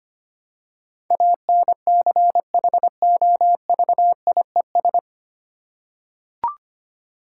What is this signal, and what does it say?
Morse code, heard as a keyed tone, about 25 wpm, 690 Hz. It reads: ANCHOVIES
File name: practice / 25wpm / Difficult Words with Hints - 25wpm